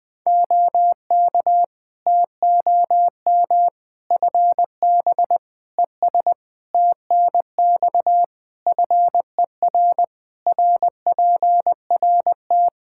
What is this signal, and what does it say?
Morse code, heard as a keyed tone, about 21 wpm, 700 Hz. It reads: OK TOM FB ES TNX FER RPRT
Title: Practice QSO - Exchange 2 - Info - 20wpm